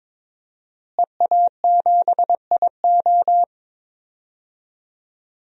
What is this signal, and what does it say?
Morse code, heard as a keyed tone, about 22 wpm, 700 Hz. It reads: EA7IO
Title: DXCC-100 Call Signs - 5 Characters - Repeated 3x - 22wpm